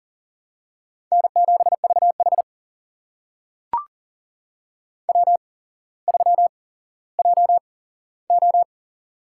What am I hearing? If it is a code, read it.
Morse code, heard as a keyed tone, about 38 wpm, spaced as for 17 wpm, 700 Hz. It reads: N7VH W3JO
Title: USA Call Signs - 4 Characters - 8x Character Spacing - 40wpm